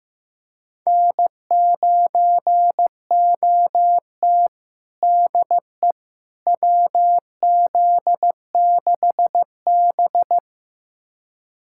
Morse code, heard as a keyed tone, about 15 wpm, 700 Hz. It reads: N9OT DE WZ6B
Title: USA Callsign DE Callsign - 4 Characters - 15wpm